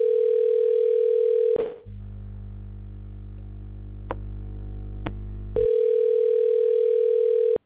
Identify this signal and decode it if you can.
Ringback tone (North America), 440 + 480 Hz.